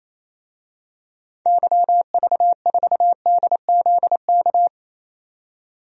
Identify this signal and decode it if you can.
Morse code, heard as a keyed tone, about 28 wpm, 700 Hz. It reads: YV4DZK
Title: DXCC-100 Call Signs - 6 Characters - 28wpm